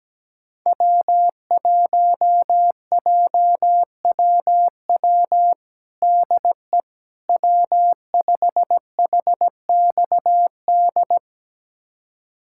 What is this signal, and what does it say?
Morse code, heard as a keyed tone, about 17 wpm, 700 Hz. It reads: W1JWW DE W5HXD
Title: USA Callsign DE Callsign 17wpm